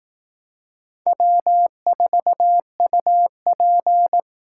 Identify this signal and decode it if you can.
Morse code, heard as a keyed tone, about 18 wpm, 700 Hz. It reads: W4UP
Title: Speed Racing - USA 1x2 Call Signs 15wpm